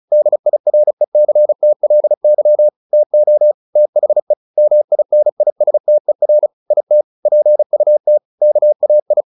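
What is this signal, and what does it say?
Morse code, heard as a keyed tone, about 34 wpm, 600 Hz. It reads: DIRECTLY TO THE MINISTER IT PUT KAI
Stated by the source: Morse practice at 35 wpm from CW med Gnister, sending Ken Follett's novel Never in English.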